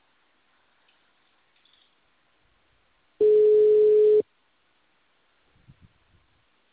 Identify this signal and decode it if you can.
Ringback tone (Europe, 425 Hz).
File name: out- -2004- - - .54851